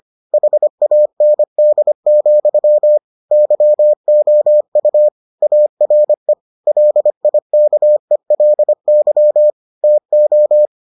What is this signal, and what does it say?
Morse code, heard as a keyed tone, about 25 wpm, 600 Hz. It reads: HAND, YOU ARE LIKELY TO